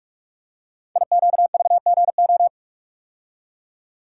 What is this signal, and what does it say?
Morse code, heard as a keyed tone, about 44 wpm, 700 Hz. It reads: IQVGO
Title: Code Groups - 5 Letters - 45wpm